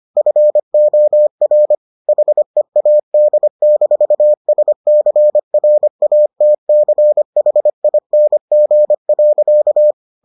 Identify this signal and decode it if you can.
Morse code, heard as a keyed tone, about 25 wpm, 600 Hz. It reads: FOR HEAD-SCRATCHING.